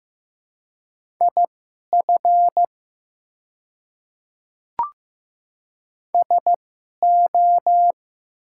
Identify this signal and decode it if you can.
Morse code, heard as a keyed tone, about 16 wpm, spaced as for 11 wpm, 700 Hz. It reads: IF SO
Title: 2x Character Spacing - Top 200 Words - 15wpm